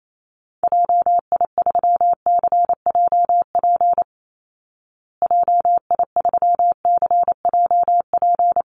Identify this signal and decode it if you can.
Morse code, heard as a keyed tone, about 28 wpm, 700 Hz. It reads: JI3CJP JI3CJP
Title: Call Signs Repeated 3x - 28wpm